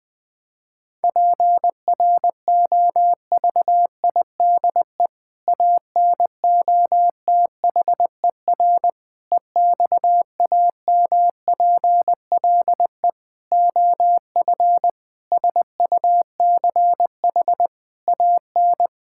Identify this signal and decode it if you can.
Morse code, heard as a keyed tone, about 19 wpm, 700 Hz. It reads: PROVIDE ANOTHER EXAMPLE OF SUCH AN
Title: Sentences from Top 1000 Words - Review Part 5 - 20wpm